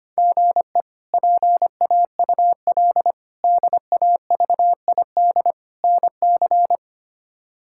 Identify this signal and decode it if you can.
Morse code, heard as a keyed tone, about 25 wpm, 700 Hz. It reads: GE PAUL DAVID NC
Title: K1USN SST QSO Practice - 25wpm